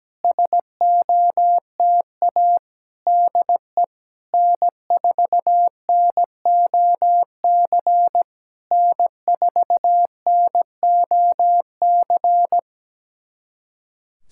Morse code, heard as a keyed tone, about 17 wpm, 700 Hz. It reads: SOTA DE N4NOC N4NOC
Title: SOTA QSO Practice 17wpm